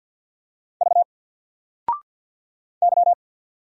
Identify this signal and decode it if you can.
Morse code, heard as a keyed tone, about 49 wpm, 700 Hz. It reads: V Y
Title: Single Character (Letter:Number:Punctuation:Prosign) 50wpm